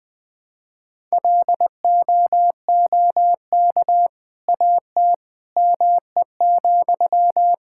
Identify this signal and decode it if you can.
Morse code, heard as a keyed tone, about 21 wpm, 700 Hz. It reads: LOOK AT ME,
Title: Sentences from Top 1000 Words - Review Part 5 - 20wpm